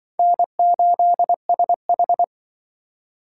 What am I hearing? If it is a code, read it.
Morse code, heard as a keyed tone, about 24 wpm, 700 Hz. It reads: N8SH
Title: Speed Racing - USA 1x2 Call Signs 20wpm